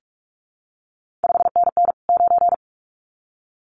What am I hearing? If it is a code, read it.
Morse code, heard as a keyed tone, about 46 wpm, 700 Hz. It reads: 5NN 9
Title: CQ WW DX Contest Exchange Encore - 45wpm